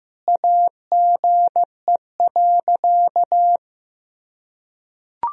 Morse code, heard as a keyed tone, about 15 wpm, 700 Hz. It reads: AGE.
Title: Sentences from Top 1000 Words - Review Part 3 - 15wpm